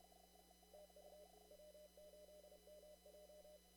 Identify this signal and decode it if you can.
Morse code, heard as a keyed tone, about 30 wpm, 600 Hz, with noise. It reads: TU W8MJ
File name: CW / kt5j wpx cw-op-k5pi